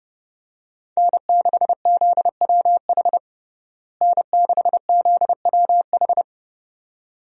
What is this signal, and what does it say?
Morse code, heard as a keyed tone, about 30 wpm, 700 Hz. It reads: N6ZWH N6ZWH
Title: USA Call Signs - 5 Characters - Repeated 2x - 30wpm